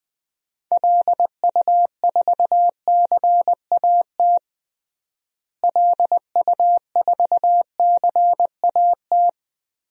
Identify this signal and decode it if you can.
Morse code, heard as a keyed tone, about 20 wpm, 700 Hz. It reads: LU4CAT LU4CAT
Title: DXCC-100 Call Signs - 6 Characters - Repeated 2x - 20wpm